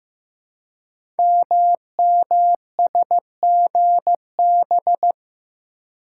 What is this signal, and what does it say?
Morse code, heard as a keyed tone, about 15 wpm, 700 Hz. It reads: MMSGB
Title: Code Groups - 5 Letters - 15wpm